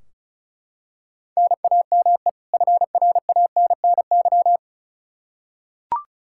Morse code, heard as a keyed tone, about 36 wpm, 700 Hz. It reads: NAME FRANNY
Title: Practice QSO - Exchange 1 - Beginning - 35wpm